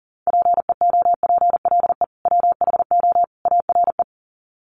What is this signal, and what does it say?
Morse code, heard as a keyed tone, about 40 wpm, 700 Hz. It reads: PEOPLE WHO ARE
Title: Sets of 3 Words 40wpm